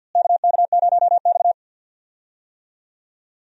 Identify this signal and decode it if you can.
Morse code, heard as a keyed tone, about 51 wpm, 700 Hz. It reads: KK0X
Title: USA Call Signs - 4 Characters - Repeated 3x - 50wpm